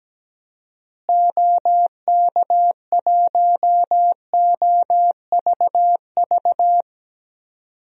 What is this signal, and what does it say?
Morse code, heard as a keyed tone, about 17 wpm, 700 Hz. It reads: OK1OVV
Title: DXCC-100 Call Signs - 6 Characters - Repeated 2x - 17wpm